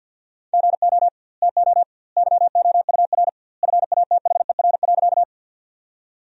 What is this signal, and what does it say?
Morse code, heard as a keyed tone, about 52 wpm, 700 Hz. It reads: GO TO YOUR FATHER.
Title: 2x Word-Spacing - Sentences from Top 300 Words - 50wpm